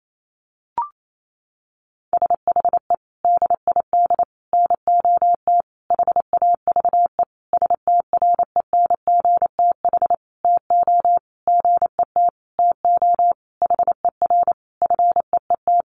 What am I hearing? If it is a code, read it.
Morse code, heard as a keyed tone, about 28 wpm, 700 Hz. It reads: SHE DID NOT HAVE STRENGTH TO GET TO HER FEET
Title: Sentences from Top 1300 Words 28wpm